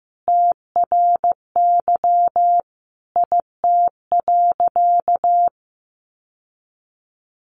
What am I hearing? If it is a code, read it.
Morse code, heard as a keyed tone, about 15 wpm, 700 Hz. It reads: TRY IT.
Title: Sentences from Top 1100 Words 15wpm